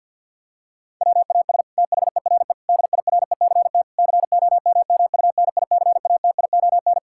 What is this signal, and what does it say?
Morse code, heard as a keyed tone, about 49 wpm, 700 Hz. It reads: WAS THERE DIRECT COMMUNICATION
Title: Sentences from Top 1000 Words - Review Part 6 - 50wpm